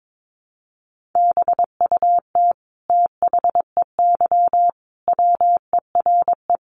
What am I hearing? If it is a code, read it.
Morse code, heard as a keyed tone, about 22 wpm, 700 Hz. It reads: BUT THEY WERE
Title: Sentences from Top 1500 Words 22wpm